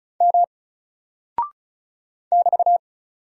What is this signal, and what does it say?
Morse code, heard as a keyed tone, about 35 wpm, 700 Hz. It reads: M <BT>
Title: Single Character (Letter:Number:Punctuation:Prosign) Alt 35wpm